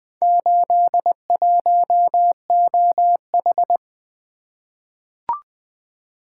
Morse code, heard as a keyed tone, about 20 wpm, 700 Hz. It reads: 81OH
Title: Four Letter-Number Combinations - 20wpm